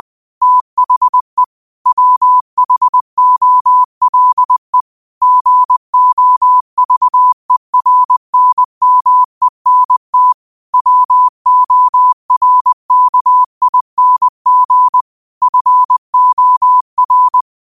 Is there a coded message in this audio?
Yes: Morse code from 0.4 to 17.4 s, heard as a keyed tone, about 20 wpm, 1000 Hz. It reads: THE WHOLE GOVERNMENT WORKING FOR